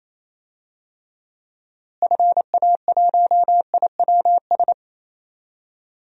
Morse code, heard as a keyed tone, about 28 wpm, 700 Hz. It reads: FA1IWS
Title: DXCC-100 Call Signs - 6 Characters - Repeated 3x - 28wpm